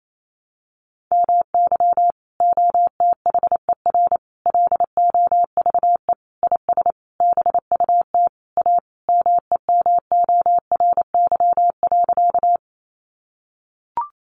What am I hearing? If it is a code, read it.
Morse code, heard as a keyed tone, about 28 wpm, 700 Hz. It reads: MY OTHER LOVE IS BUT A MEMORY.
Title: Sentences from Top 1300 Words 28wpm